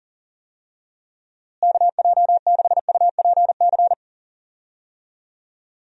Morse code, heard as a keyed tone, about 37 wpm, 700 Hz. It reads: KJ6UPC
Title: USA Call Signs - 6 Characters - Repeated 2x - 40wpm